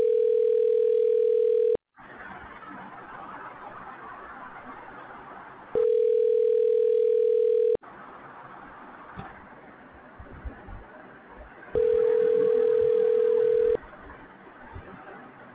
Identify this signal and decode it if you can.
Ringback tone (North America), 440 + 480 Hz.